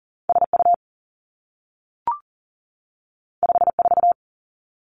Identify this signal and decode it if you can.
Morse code, heard as a keyed tone, about 42 wpm, 700 Hz. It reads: S U 5 4
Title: Two Challenging Letter-Number Combinations 40wpm